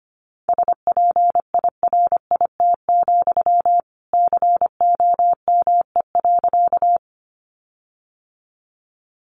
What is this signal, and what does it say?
Morse code, heard as a keyed tone, about 25 wpm, 700 Hz. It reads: SPIRIT, COME.
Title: Sentences from Top 1400 Words 25wpm